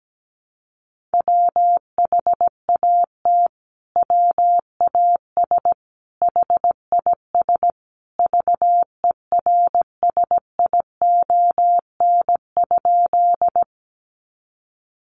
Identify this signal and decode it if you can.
Morse code, heard as a keyed tone, about 17 wpm, 700 Hz. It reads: WHAT WAS HIS VERSION?
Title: Sentences from Top 1400 Words 17wpm